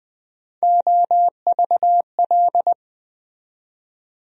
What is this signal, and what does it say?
Morse code, heard as a keyed tone, about 20 wpm, 700 Hz. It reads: OVL
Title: USA Call Sign Postfixes - 3 Letters 20wpm